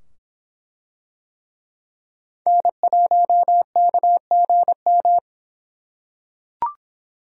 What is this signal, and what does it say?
Morse code, heard as a keyed tone, about 24 wpm, 700 Hz. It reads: N1KGM